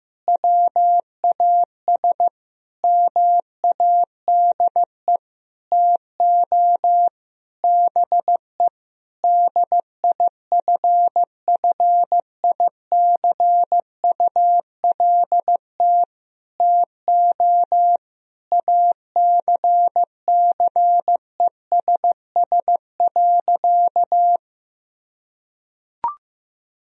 Morse code, heard as a keyed tone, about 15 wpm, 700 Hz. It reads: WAS MADE TO BE DIFFICULT TO ACCESS.
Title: Sentences from Top 1000 Words - Review Part 4 - 15wpm